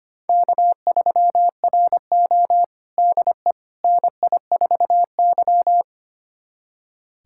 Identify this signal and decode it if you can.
Morse code, heard as a keyed tone, about 25 wpm, 700 Hz. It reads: K3RO DE NI4Y
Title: USA Callsign DE Callsign - 4 Characters - 25wpm